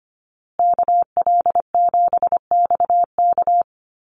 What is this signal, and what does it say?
Morse code, heard as a keyed tone, about 25 wpm, 700 Hz. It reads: KL7XK
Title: USA Call Signs Repeated 2x 25wpm